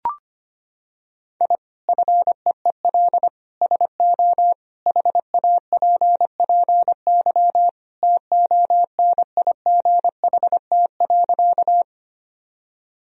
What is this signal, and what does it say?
Morse code, heard as a keyed tone, about 25 wpm, 700 Hz. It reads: I FEEL SO HAPPY TONIGHT.
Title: Sentences from Top 1000 Words - Review Part 3 - 25wpm